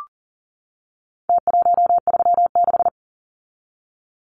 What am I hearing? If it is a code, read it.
Morse code, heard as a keyed tone, about 39 wpm, 700 Hz. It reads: T136
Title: Numbers - Contest Style - 4-Digit 40wpm